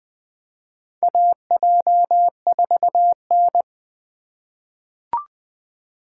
Morse code, heard as a keyed tone, about 20 wpm, 700 Hz. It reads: AJ4N
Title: USA Call Signs - 4x Character Spacing - 20wpm